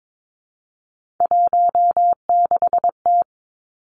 Morse code, heard as a keyed tone, about 22 wpm, 700 Hz. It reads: 16T
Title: Three Challenging Letter-Number Combinations 22wpm